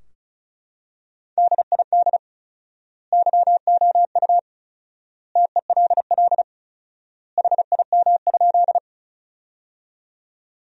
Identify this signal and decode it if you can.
Morse code, heard as a keyed tone, about 36 wpm, 700 Hz. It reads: DID YOU TELL HIM?
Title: 4x Word-Spacing - Sentences from Top 200 Words - 35wpm